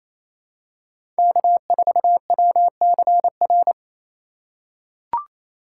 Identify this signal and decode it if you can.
Morse code, heard as a keyed tone, about 28 wpm, 700 Hz. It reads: K4WCR
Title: USA Call Signs - 5 Characters - 2x Character Spacing - 28wpm